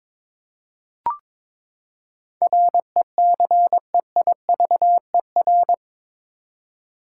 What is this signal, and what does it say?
Morse code, heard as a keyed tone, about 21 wpm, 700 Hz. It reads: RECEIVER
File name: practice / 22wpm / Eight Letter Words - 22wpm